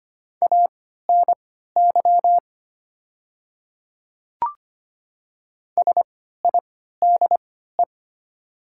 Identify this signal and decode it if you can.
Morse code, heard as a keyed tone, about 25 wpm, spaced as for 14 wpm, 690 Hz. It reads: ANY SIDE
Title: 3x Character Spacing - Top 300 Words - 25wpm